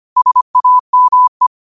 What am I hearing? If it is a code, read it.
Morse code, heard as a keyed tone, about 25 wpm, 1000 Hz. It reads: SAME
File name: qotdfiles / 25wpm